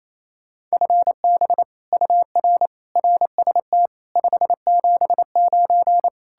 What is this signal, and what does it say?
Morse code, heard as a keyed tone, about 38 wpm, 700 Hz. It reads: FB UR RST 579